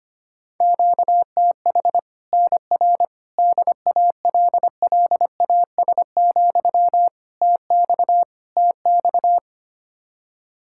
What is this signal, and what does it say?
Morse code, heard as a keyed tone, about 25 wpm, 700 Hz. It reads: QTH NR DALLAS, TX TX
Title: Practice QSO - Exchange 1 - Beginning - 25wpm